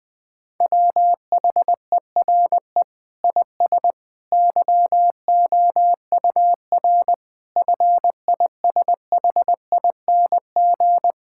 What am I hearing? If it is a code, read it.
Morse code, heard as a keyed tone, about 20 wpm, 700 Hz. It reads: WHERE IS YOUR FISHING
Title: Sentences from Top 1600 Words 20wpm